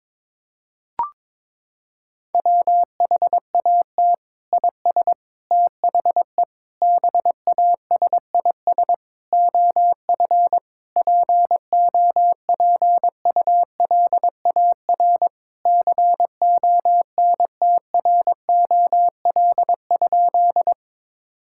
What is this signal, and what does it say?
Morse code, heard as a keyed tone, about 21 wpm, 700 Hz. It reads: WHAT IS THE BASIS OF POPULAR CONTROL?
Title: Sentences from Top 1000 Words - Review Part 5 - 22wpm